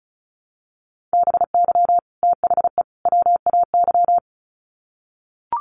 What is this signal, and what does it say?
Morse code, heard as a keyed tone, about 34 wpm, 700 Hz. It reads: BY THE WAY
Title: Sets of 3 Words 35wpm